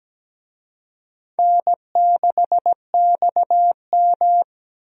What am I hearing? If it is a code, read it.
Morse code, heard as a keyed tone, about 17 wpm, 700 Hz. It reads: N6XM
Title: USA Call Signs - 4 Characters - Repeated 2x - 17wpm